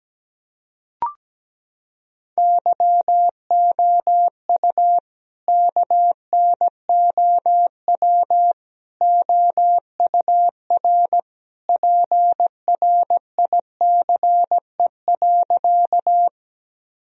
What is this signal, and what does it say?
Morse code, heard as a keyed tone, about 17 wpm, 700 Hz. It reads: YOU KNOW OUR PRICE.
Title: Sentences from Top 1000 Words - Review Part 4 - 17wpm